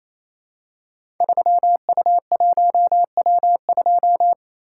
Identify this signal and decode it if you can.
Morse code, heard as a keyed tone, about 29 wpm, 700 Hz. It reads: 3U1W2
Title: Code Groups - 5 Letter:Numbers - 28wpm